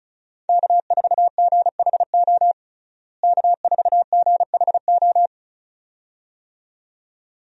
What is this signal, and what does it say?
Morse code, heard as a keyed tone, about 36 wpm, 700 Hz. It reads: K4GHO K4GHO
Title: USA Call Signs - 5 Characters - Repeated 2x - 35wpm